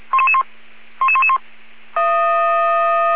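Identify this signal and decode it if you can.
Five-tone selective call (ZVEI), heard as quick runs of five tones, tones of about 70 ms. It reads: 11111
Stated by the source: ZVEI five-tone selective call 77777 received by SDR.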